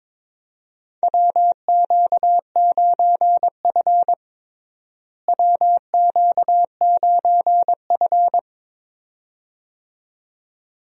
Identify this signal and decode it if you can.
Morse code, heard as a keyed tone, about 22 wpm, 700 Hz. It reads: WQ9F WQ9F
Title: USA Call Signs - 4 Characters - Repeated 2x - 22wpm